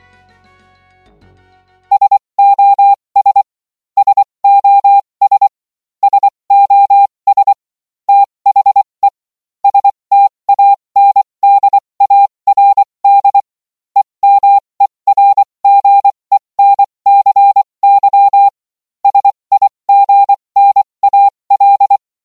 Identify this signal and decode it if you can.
Morse code, heard as a keyed tone, about 24 wpm, spaced as for 20 wpm, 800 Hz. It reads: SOS SOS SOS THE STANDARD EMERGENCY SIGNAL